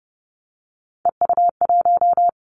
Morse code, heard as a keyed tone, about 30 wpm, 700 Hz. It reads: EU1
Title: Callsign Prefixes for Top 50 DXCC - 30wpm